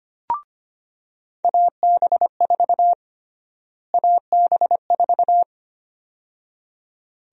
Morse code, heard as a keyed tone, about 25 wpm, 700 Hz. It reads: AB4 AB4
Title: USA Call Sign Prefixes - Repeated 2x - 25wpm